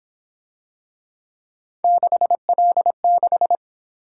Morse code, heard as a keyed tone, about 26 wpm, 700 Hz. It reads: 6L6
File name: Speed-Racing - Three Letter-Number Combinations - 22wpm